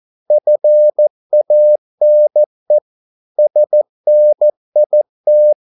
Morse code, heard as a keyed tone, about 14 wpm, 600 Hz. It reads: FANE SNIT
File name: Korte ord DK 14wpm